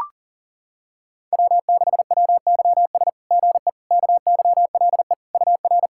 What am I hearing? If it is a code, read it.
Morse code, heard as a keyed tone, about 37 wpm, 690 Hz. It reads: W6WYS GE KYLE UR
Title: SOTA QSO Practice 40wpm